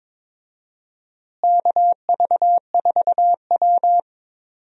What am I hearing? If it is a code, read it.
Morse code, heard as a keyed tone, about 22 wpm, 700 Hz. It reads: KV4W
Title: USA Call Signs - 4 Characters - Repeated 3x - 22wpm